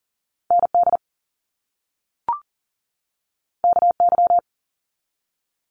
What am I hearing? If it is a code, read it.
Morse code, heard as a keyed tone, about 39 wpm, 700 Hz. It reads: ND KY
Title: US States 40wpm